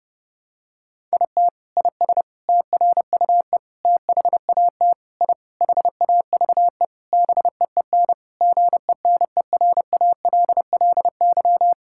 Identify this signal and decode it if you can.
Morse code, heard as a keyed tone, about 30 wpm, 700 Hz. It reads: IT IS TRUE THAT I HAVE BEEN GENERALLY
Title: Sentences from Top 1000 Words - Review Part 6 - 30wpm